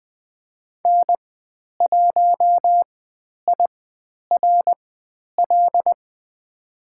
Morse code, heard as a keyed tone, about 20 wpm, spaced as for 10 wpm, 700 Hz. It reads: N1IRL
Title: 095 Lesson 16g - Letter c - Call Signs 20wpm